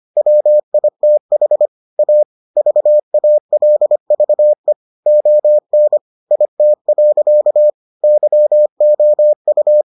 Morse code, heard as a keyed tone, about 25 wpm, 600 Hz. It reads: WITH A VALVE ON IT. YOU